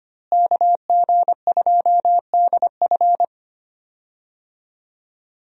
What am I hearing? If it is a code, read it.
Morse code, heard as a keyed tone, about 25 wpm, 700 Hz. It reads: KG2DF
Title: USA Call Signs - 5 Characters - Repeated 3x - 25wpm